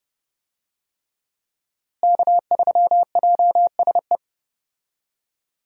Morse code, heard as a keyed tone, about 30 wpm, 700 Hz. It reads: K3JSE
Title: USA Call Signs - 5 Characters - Repeated 3x - 30wpm